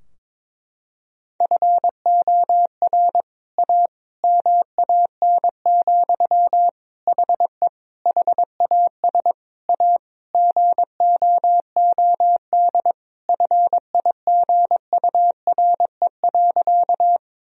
Morse code, heard as a keyed tone, about 22 wpm, 700 Hz. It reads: FOR A MAN, HE HAS A GOOD FIGURE.
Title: Sentences from Top 1000 Words - Review Part 2 - 22wpm